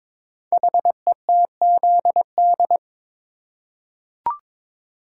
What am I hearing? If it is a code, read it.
Morse code, heard as a keyed tone, about 22 wpm, 700 Hz. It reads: HETZD
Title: Code Groups - 5 Letters - 22wpm